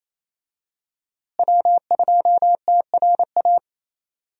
Morse code, heard as a keyed tone, about 28 wpm, 700 Hz. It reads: W2TRA
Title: USA Call Signs - 5 Characters - Repeated 3x - 28wpm